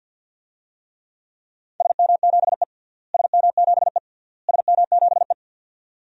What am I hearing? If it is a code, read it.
Morse code, heard as a keyed tone, about 46 wpm, 700 Hz. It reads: SM7E SM7E SM7E
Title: DXCC-100 Call Signs - 4 Characters - Repeated 3x - 50wpm